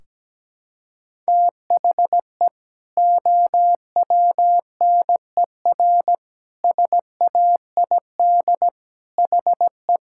Morse code, heard as a keyed tone, about 17 wpm, 700 Hz. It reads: THE OWNER SAID HE
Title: Sentences from Top 1600 Words 17wpm